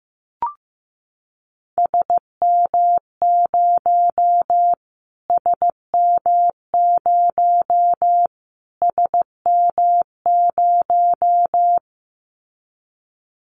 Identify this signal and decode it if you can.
Morse code, heard as a keyed tone, about 15 wpm, 700 Hz. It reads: SM0 SM0 SM0
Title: Callsign Prefixes for Top 25 DXCC - Repeated 3x - 15wpm